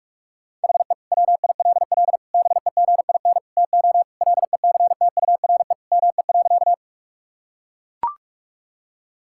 Morse code, heard as a keyed tone, about 41 wpm, 700 Hz. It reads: HE WILL BEGIN TO LECTURE ME.
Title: Sentences from Top 1600 Words 45wpm